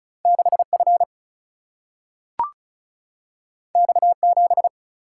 Morse code, heard as a keyed tone, about 36 wpm, 700 Hz. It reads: 6F X7
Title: Single-Letter Single-Number 35wpm